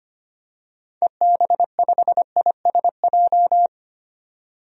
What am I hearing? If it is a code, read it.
Morse code, heard as a keyed tone, about 25 wpm, 700 Hz. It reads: EB5ISJ
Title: DXCC-100 Call Signs - 6 Characters - 25wpm